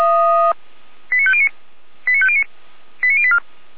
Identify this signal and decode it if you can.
Five-tone selective call (ZVEI), heard as quick runs of five tones, tones of about 70 ms. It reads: 89509 89509 89084
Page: /455/SDR/ZVEI/